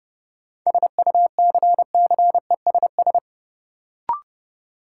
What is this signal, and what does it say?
Morse code, heard as a keyed tone, about 32 wpm, 700 Hz. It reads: SUCCESS
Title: Sentences from Top 900 Words - New Word Drill - 30wpm